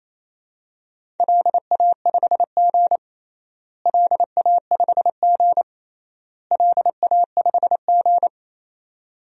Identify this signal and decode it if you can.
Morse code, heard as a keyed tone, about 28 wpm, 700 Hz. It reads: LA5G LA5G LA5G
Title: DXCC-100 Call Signs - 4 Characters - Repeated 3x - 28wpm